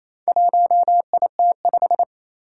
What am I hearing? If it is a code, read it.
Morse code, heard as a keyed tone, about 27 wpm, 700 Hz. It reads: 1IT5
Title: Four Challenging Letter-Number Combinations 28wpm